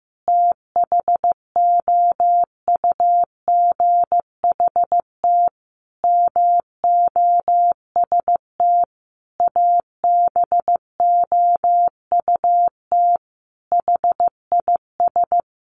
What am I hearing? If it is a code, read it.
Morse code, heard as a keyed tone, about 15 wpm, 700 Hz. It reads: THOUGHT MOST ABOUT HIS
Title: Sentences from Top 1500 Words 15wpm